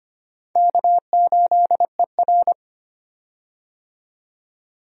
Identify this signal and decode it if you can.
Morse code, heard as a keyed tone, about 25 wpm, 700 Hz. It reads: K8ER
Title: USA Call Signs - 4 Characters - 25wpm